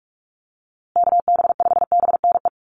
Morse code, heard as a keyed tone, about 44 wpm, 700 Hz. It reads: KB5BNE
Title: USA Call Signs Repeated 3x 45wpm